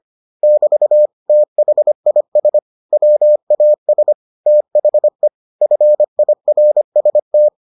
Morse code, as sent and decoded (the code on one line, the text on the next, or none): -...- / - .... .. ... / .-- .- ... / - .... / ..-. .. .-. ... -
<BT> THIS WAS THE FIRST